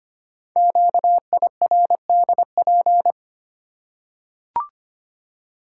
Morse code, heard as a keyed tone, about 25 wpm, 700 Hz. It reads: QIRDP
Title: Code Groups - 5 Letter:Numbers - 25wpm